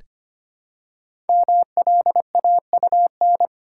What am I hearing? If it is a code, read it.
Morse code, heard as a keyed tone, about 25 wpm, 700 Hz. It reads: MLAUN
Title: Code Groups - 5 Letters - 25wpm